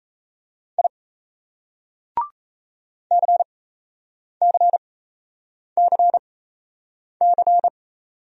Morse code, heard as a keyed tone, about 35 wpm, spaced as for 8 wpm, 700 Hz. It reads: I CCCC